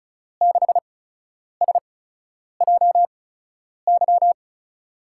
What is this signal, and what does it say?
Morse code, heard as a keyed tone, about 36 wpm, 700 Hz. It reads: 6 S J Y
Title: USA Call Signs - 6 Characters - 8x Character Spacing - 35wpm